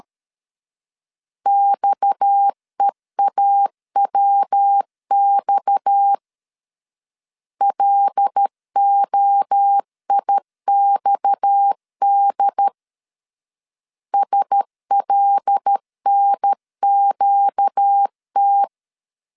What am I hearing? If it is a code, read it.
Morse code, heard as a keyed tone, about 13 wpm, 800 Hz. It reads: XEAWX LOIXD SLNQT
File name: LEZIONI / 6 PKX0 / 6 PKX 12wpm 2024